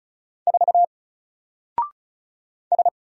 Morse code, heard as a keyed tone, about 36 wpm, 700 Hz. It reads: 4 S
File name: Single-Letter-Number-35wpm